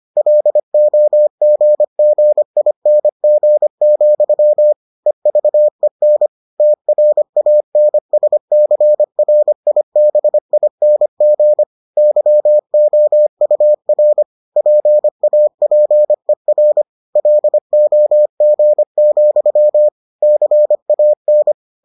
Morse code, heard as a keyed tone, about 25 wpm, 600 Hz. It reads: LOGGING, EVEN TRANSCRIBING YOUR PAPER LOG, CAN